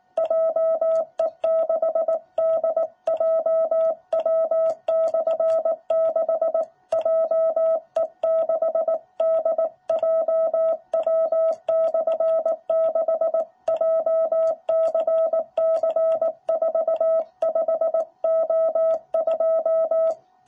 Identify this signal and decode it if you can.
Morse code, heard as a keyed tone, about 18 wpm, 660 Hz. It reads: JE6DJW/6 JE6DJW/6 JCC45O2
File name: jh8cbh / 2021cw